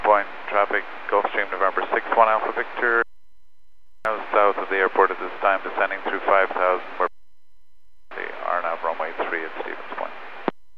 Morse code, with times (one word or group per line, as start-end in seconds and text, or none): none